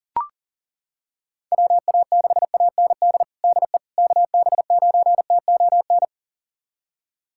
Morse code, heard as a keyed tone, about 40 wpm, 700 Hz. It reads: WA6AND DE KB9TON
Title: USA Callsign DE Callsign 40wpm